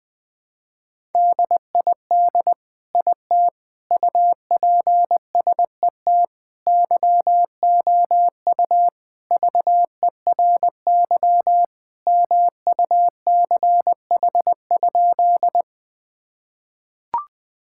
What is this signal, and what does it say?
Morse code, heard as a keyed tone, about 20 wpm, 700 Hz. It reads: DID IT UPSET YOU VERY MUCH?
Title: Sentences from Top 1000 Words - Review Part 5 - 20wpm